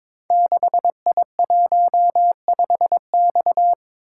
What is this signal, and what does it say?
Morse code, heard as a keyed tone, about 22 wpm, 700 Hz. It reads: 6I15X
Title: Code Groups - 5 Letter:Numbers - 22wpm